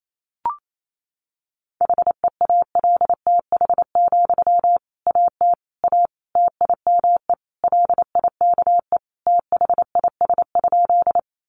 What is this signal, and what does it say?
Morse code, heard as a keyed tone, about 28 wpm, 700 Hz. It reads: HEALTH, AT A TIME LIKE THIS?
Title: Sentences from Top 600 Words 28wpm